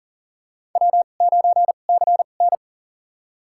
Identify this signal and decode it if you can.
Morse code, heard as a keyed tone, about 40 wpm, spaced as for 29 wpm, 700 Hz. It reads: W9CN
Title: USA Call Signs - 2x Character Spacing - 40wpm